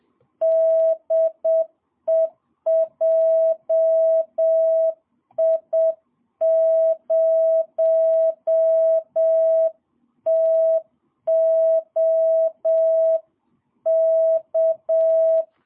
Morse code, heard as a keyed tone, about 7 wpm, 650 Hz. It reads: DE JI0TOK
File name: jh8cbh / 2022cw